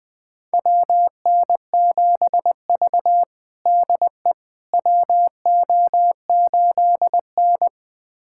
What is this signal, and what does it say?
Morse code, heard as a keyed tone, about 20 wpm, 700 Hz. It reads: WN7V DE WO8N
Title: USA Callsign DE Callsign - 4 Characters - 20wpm